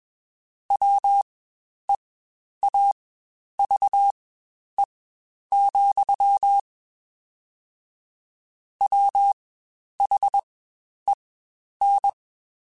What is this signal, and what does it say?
Morse code, heard as a keyed tone, about 21 wpm, spaced as for 10 wpm, 800 Hz. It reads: WEAVE, WHEN
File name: Quotes Spaced / Quotes Spaced-mp3s / Deceive sp @20